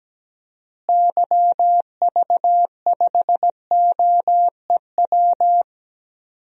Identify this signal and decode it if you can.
Morse code, heard as a keyed tone, about 17 wpm, 700 Hz. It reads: YV5OEW
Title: DXCC-100 Call Signs - 6 Characters - Repeated 2x - 17wpm